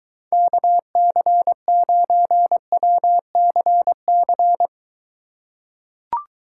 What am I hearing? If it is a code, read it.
Morse code, heard as a keyed tone, about 23 wpm, 700 Hz. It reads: KC9WCC